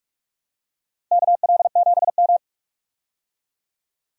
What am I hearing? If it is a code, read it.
Morse code, heard as a keyed tone, about 46 wpm, 700 Hz. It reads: KL7M
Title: DXCC-100 Call Signs - 4 Characters - Repeated 3x - 45wpm